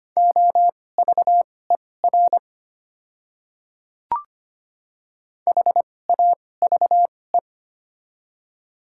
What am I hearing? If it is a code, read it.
Morse code, heard as a keyed tone, about 25 wpm, spaced as for 18 wpm, 700 Hz. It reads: OVER HAVE
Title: 2x Character Spacing - Top 200 Words - 25wpm